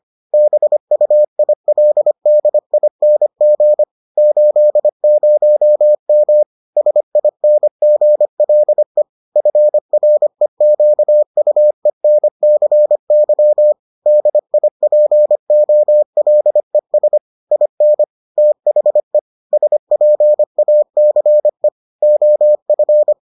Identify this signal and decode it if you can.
Morse code, heard as a keyed tone, about 25 wpm, 600 Hz. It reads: BUILDING 80M SINGLE FREQUENCY DIPOLES IN THE SPACE OF